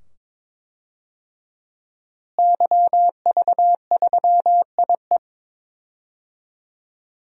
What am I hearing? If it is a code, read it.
Morse code, heard as a keyed tone, about 22 wpm, 700 Hz. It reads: YV3IE
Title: DXCC-100 Call Signs - 5 Characters - Repeated 2x - 22wpm